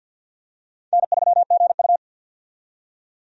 Morse code, heard as a keyed tone, about 52 wpm, 700 Hz. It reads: N3GU
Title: USA Call Signs - 4 Characters - 50wpm